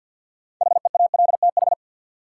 Morse code, heard as a keyed tone, about 52 wpm, 700 Hz. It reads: HEALTH
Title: Sets of 5 Words - New Word Drill - 50wpm